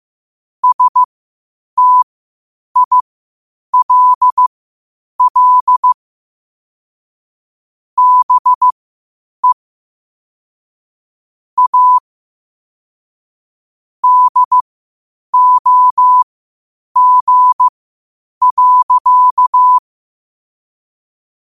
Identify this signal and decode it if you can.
Morse code, heard as a keyed tone, about 15 wpm, spaced as for 8 wpm, 1000 Hz. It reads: STILL BE A DOG.
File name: qotdfiles / 8wpm